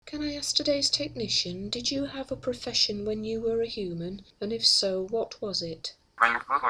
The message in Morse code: none